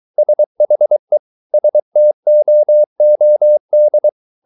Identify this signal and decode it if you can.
Morse code, heard as a keyed tone, about 23 wpm, 600 Hz. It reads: SHE STOOD